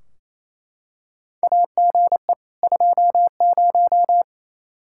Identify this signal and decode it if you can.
Morse code, heard as a keyed tone, about 28 wpm, 700 Hz. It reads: AGE 20
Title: Practice QSO - Exchange 3 - More Info - 28wpm